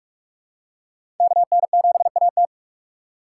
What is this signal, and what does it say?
Morse code, heard as a keyed tone, about 44 wpm, 700 Hz. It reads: KN7AT
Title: USA Call Signs - 3x Character Spacing - 45wpm